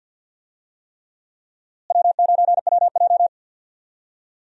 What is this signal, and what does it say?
Morse code, heard as a keyed tone, about 47 wpm, 700 Hz. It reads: W9WJ
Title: USA Call Signs - 4 Characters - Repeated 3x - 50wpm